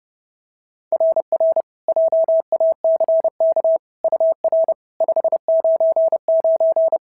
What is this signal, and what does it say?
Morse code, heard as a keyed tone, about 30 wpm, 690 Hz. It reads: RR JACK UR 599
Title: SOTA QSO Practice 30wpm